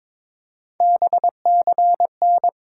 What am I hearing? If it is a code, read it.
Morse code, heard as a keyed tone, about 22 wpm, 700 Hz. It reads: BCN
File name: Top 90 QSO Elements - 8x Character Spacing - 22wpm